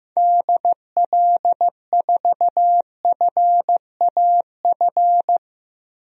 Morse code, heard as a keyed tone, about 15 wpm, 700 Hz. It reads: DL4FAF
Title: DXCC-100 Call Signs - 6 Characters - Repeated 3x - 15wpm